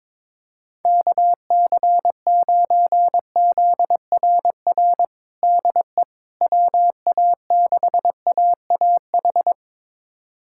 Morse code, heard as a keyed tone, about 22 wpm, 700 Hz. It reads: KC9ZRR DE WA6AAH
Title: USA Callsign DE Callsign 22wpm